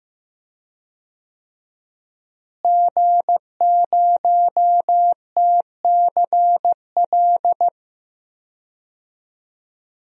Morse code, heard as a keyed tone, about 15 wpm, 700 Hz. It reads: G0TCL
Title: DXCC-100 Call Signs - 5 Characters - Repeated 2x - 15wpm